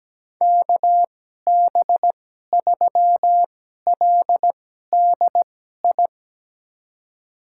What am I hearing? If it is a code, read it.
Morse code, heard as a keyed tone, about 17 wpm, spaced as for 12 wpm, 700 Hz. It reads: KB3LDI
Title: USA Call Signs - 2x Character Spacing - 17wpm